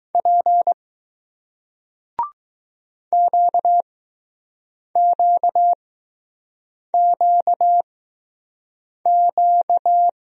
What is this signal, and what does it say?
Morse code, heard as a keyed tone, about 20 wpm, spaced as for 6 wpm, 700 Hz. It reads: P QQQQ